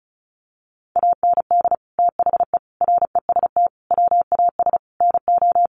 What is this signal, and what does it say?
Morse code, heard as a keyed tone, about 34 wpm, 700 Hz. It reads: AND THE REST WAS NO
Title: Sentences from Top 1000 Words 35wpm